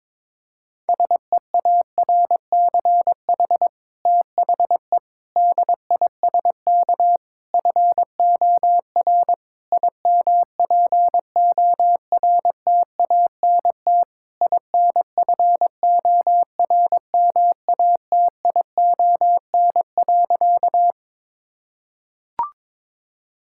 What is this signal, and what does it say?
Morse code, heard as a keyed tone, about 22 wpm, 700 Hz. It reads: SEARCH THE DISK FOR IMPORTANT INFORMATION.
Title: Sentences from Top 1000 Words - Review Part 6 - 22wpm